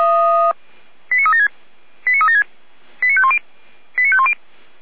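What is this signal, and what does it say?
Five-tone selective call (ZVEI), heard as quick runs of five tones, tones of about 70 ms. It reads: 89376 89376 89510 89510
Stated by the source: Five-tone ZVEI selective call received by SDR, from feuerwehr-orlamuende.org fire brigade audio archive.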